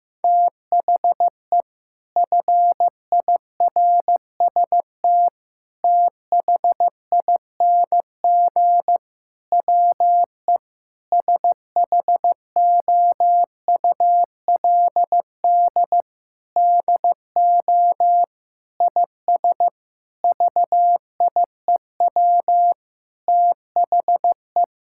Morse code, heard as a keyed tone, about 15 wpm, 700 Hz. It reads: THE FIRST THING WE SHOULD DO IS VIEW THE
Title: Sentences from Top 1000 Words - Review Part 6 - 15wpm